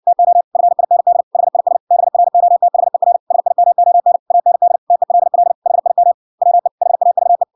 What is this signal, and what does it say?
Morse code, heard as a keyed tone, about 58 wpm, 700 Hz. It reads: TO FIND HIS BROTHER SIMON AND TELL HIM WE HAVE